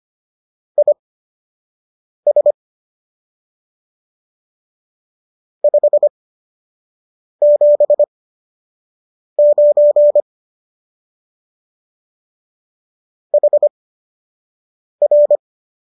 Morse code, heard as a keyed tone, about 25 wpm, spaced as for 6 wpm, 590 Hz. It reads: IS 579 HR